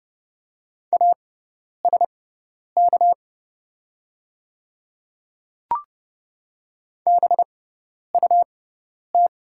Morse code, heard as a keyed tone, about 30 wpm, spaced as for 10 wpm, 700 Hz. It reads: ASK BUT
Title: 6x Character Spacing - Top 200 Words - 30wpm